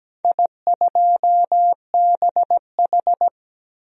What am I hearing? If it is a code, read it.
Morse code, heard as a keyed tone, about 17 wpm, 700 Hz. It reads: I2BH
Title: DXCC-100 Call Signs - 4 Characters - 17wpm